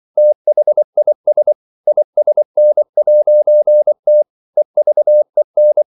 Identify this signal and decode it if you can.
Morse code, heard as a keyed tone, about 24 wpm, 600 Hz. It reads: THIS ISN'T EVEN
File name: CW / Practice / EnglishQuotations0424